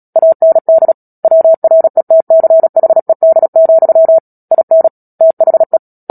Morse code, heard as a keyed tone, about 37 wpm, 650 Hz. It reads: AND WRETCHED, IN THE
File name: Book Two / 10-Chapter 10 - Epilogue - 36 WPM